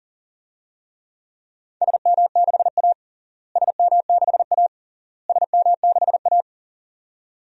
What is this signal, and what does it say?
Morse code, heard as a keyed tone, about 40 wpm, 700 Hz. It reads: SM6A SM6A SM6A
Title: DXCC-100 Call Signs - 4 Characters - Repeated 3x - 40wpm